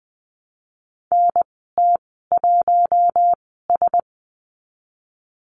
Morse code, heard as a keyed tone, about 20 wpm, spaced as for 15 wpm, 700 Hz. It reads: NT1S
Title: 042 Lesson 8g - Number 1 - Call Signs 20wpm